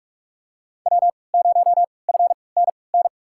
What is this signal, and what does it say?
Morse code, heard as a keyed tone, about 43 wpm, spaced as for 25 wpm, 700 Hz. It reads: W0FNN
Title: USA Call Signs - 5 Characters - 3x Character Spacing - 45wpm